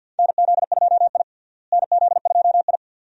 Morse code, heard as a keyed tone, about 47 wpm, 700 Hz. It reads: NZ2I NZ2I
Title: USA Call Signs - 4 Characters - Repeated 2x - 50wpm